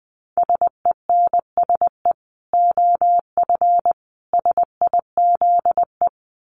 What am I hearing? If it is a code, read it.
Morse code, heard as a keyed tone, about 20 wpm, 700 Hz. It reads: SENSE OF SIZE
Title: Dit Challenge 20wpm